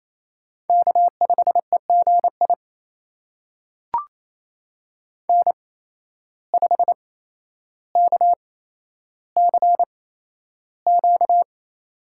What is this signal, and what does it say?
Morse code, heard as a keyed tone, about 28 wpm, spaced as for 12 wpm, 700 Hz. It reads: K5EGI N5KCQ
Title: USA Call Signs - 5 Characters - 8x Character Spacing - 28wpm